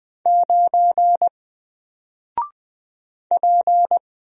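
Morse code, heard as a keyed tone, about 20 wpm, 700 Hz. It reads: 9 P
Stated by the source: Single letter and number Morse drill at 20 wpm, from N4LCD.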